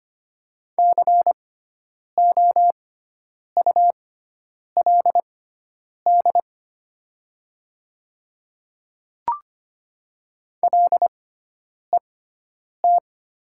Morse code, heard as a keyed tone, about 25 wpm, spaced as for 9 wpm, 700 Hz. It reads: COULD LET
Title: 6x Character Spacing - Top 500 Words - 25wpm